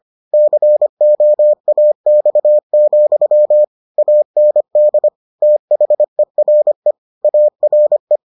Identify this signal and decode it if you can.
Morse code, heard as a keyed tone, about 25 wpm, 600 Hz. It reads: COAX, AND THERE ARE